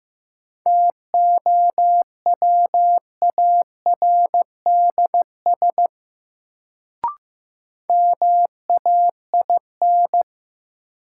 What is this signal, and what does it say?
Morse code, heard as a keyed tone, about 15 wpm, 700 Hz. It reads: TOWARDS MAIN
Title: Sentences from Top 1000 Words - Review Part 2 - New Word Drill - 15wpm